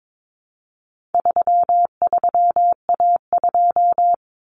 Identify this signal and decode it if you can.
Morse code, heard as a keyed tone, about 22 wpm, 700 Hz. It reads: 33A2
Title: Common Cut Numbers - 4-Digit 22wpm